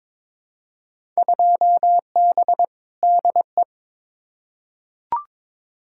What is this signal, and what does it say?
Morse code, heard as a keyed tone, about 22 wpm, 700 Hz. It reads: 2B DE
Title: Field Day Exchanges Encore - 22wpm